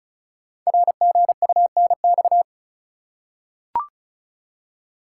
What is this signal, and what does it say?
Morse code, heard as a keyed tone, about 36 wpm, 690 Hz. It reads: RGUNX
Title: Code Groups - 5 Letters - 35wpm